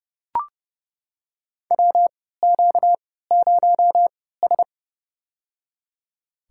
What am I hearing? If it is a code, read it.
Morse code, heard as a keyed tone, about 30 wpm, spaced as for 17 wpm, 700 Hz. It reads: WQ0S
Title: USA Call Signs - 4 Characters - 3x Character Spacing - 30wpm